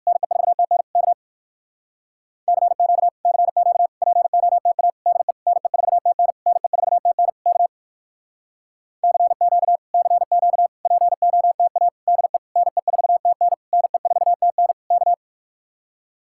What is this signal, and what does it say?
Morse code, heard as a keyed tone, about 59 wpm, 700 Hz. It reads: NE4TN K CQ CQ POTA DE NE4TN NE4TN K CQ CQ POTA DE NE4TN NE4TN K